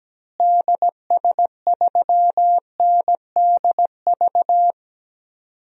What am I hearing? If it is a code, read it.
Morse code, heard as a keyed tone, about 17 wpm, 700 Hz. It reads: DS3NDV
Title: DXCC-100 Call Signs - 6 Characters - Repeated 2x - 17wpm